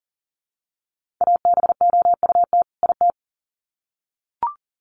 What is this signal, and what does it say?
Morse code, heard as a keyed tone, about 40 wpm, 700 Hz. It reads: ABOUT IT
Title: Sets of 2 Words 40wpm